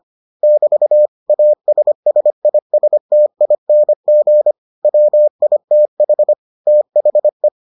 Morse code, heard as a keyed tone, about 25 wpm, 600 Hz. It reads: <BT> ASSISTING WITH THE